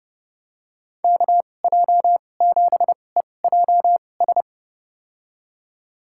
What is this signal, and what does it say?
Morse code, heard as a keyed tone, about 30 wpm, spaced as for 22 wpm, 700 Hz. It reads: KJ7EJS
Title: USA Call Signs - 6 Characters - 2x Character Spacing - 30wpm